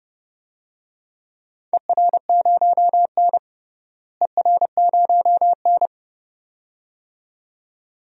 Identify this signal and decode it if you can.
Morse code, heard as a keyed tone, about 30 wpm, 700 Hz. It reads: ER0N ER0N
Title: DXCC-100 Call Signs - 4 Characters - Repeated 2x - 30wpm